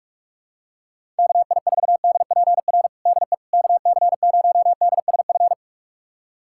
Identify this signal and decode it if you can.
Morse code, heard as a keyed tone, about 46 wpm, 700 Hz. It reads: KI4DPR DE KC0DSF
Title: USA Callsign DE Callsign 45wpm